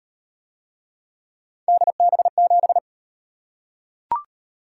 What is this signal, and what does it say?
Morse code, heard as a keyed tone, about 33 wpm, 700 Hz. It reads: DB7